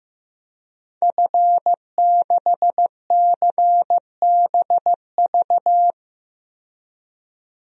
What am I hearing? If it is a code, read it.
Morse code, heard as a keyed tone, about 15 wpm, 700 Hz. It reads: F6CBV
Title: DXCC-100 Call Signs - 5 Characters - 15wpm